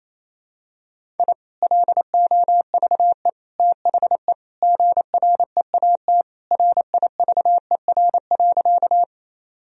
Morse code, heard as a keyed tone, about 28 wpm, 700 Hz. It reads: I LOVE THE GREAT RIVER.
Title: Sentences from Top 1000 Words - Review Part 1 - 28wpm